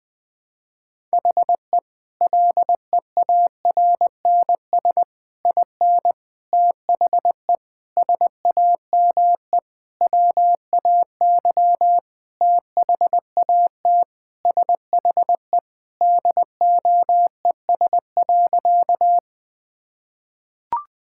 Morse code, heard as a keyed tone, about 20 wpm, 700 Hz. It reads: HE LEARNS IN THE SAME WAY THAT SHE DOES.
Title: Sentences with 5-Word Phrases - 20wpm